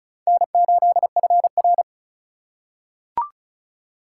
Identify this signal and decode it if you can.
Morse code, heard as a keyed tone, about 34 wpm, 700 Hz. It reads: N8FR
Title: USA Call Signs - 4 Characters - 4x Character Spacing - 35wpm